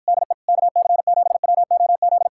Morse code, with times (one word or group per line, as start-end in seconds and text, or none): 0.1-0.3 DE
0.5-2.3 KC7WOZ